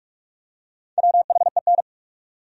Morse code, heard as a keyed tone, about 42 wpm, 700 Hz. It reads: WHEN
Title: Top 100 Words 45wpm